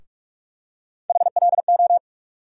Morse code, heard as a keyed tone, about 47 wpm, 700 Hz. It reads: HLO
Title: USA Call Sign Postfixes - 3 Letters 45wpm